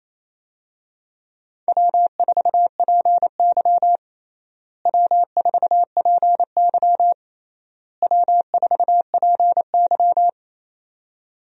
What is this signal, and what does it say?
Morse code, heard as a keyed tone, about 28 wpm, 700 Hz. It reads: W4PY W4PY W4PY
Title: USA Call Signs - 4 Characters - Repeated 3x - 28wpm